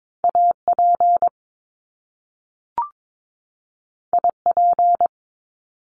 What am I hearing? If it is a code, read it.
Morse code, heard as a keyed tone, about 22 wpm, 700 Hz. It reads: AP IP
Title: Top Two Letter Combinations in Words 22wpm